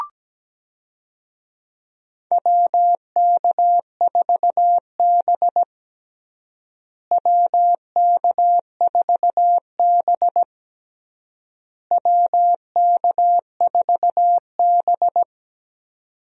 Morse code, heard as a keyed tone, about 17 wpm, 700 Hz. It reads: WK4B WK4B WK4B
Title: USA Call Signs - 4 Characters - Repeated 3x - 17wpm